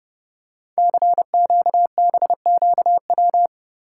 Morse code, heard as a keyed tone, about 30 wpm, 700 Hz. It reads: CQBQW
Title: Code Groups - 5 Letter:Numbers - 30wpm